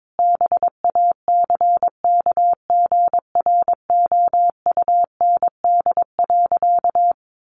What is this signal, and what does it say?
Morse code, heard as a keyed tone, about 22 wpm, 700 Hz. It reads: BACKGROUND.
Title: Sentences from Top 1500 Words 22wpm